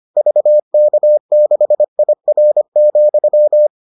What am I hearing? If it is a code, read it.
Morse code, heard as a keyed tone, about 25 wpm, 600 Hz. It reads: VK6IR,